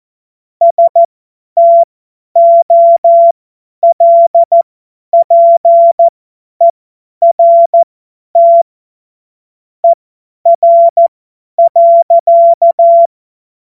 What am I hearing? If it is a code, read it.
Morse code, heard as a keyed tone, about 14 wpm, spaced as for 10 wpm, 680 Hz. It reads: STOLPERT ER.
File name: Morsekurs / Witze / Witz 14. Februar 2023 50 F70